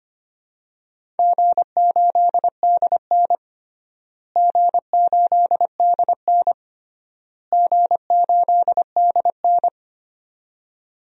Morse code, heard as a keyed tone, about 25 wpm, 700 Hz. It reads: G8DN G8DN G8DN
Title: DXCC-100 Call Signs - 4 Characters - Repeated 3x - 25wpm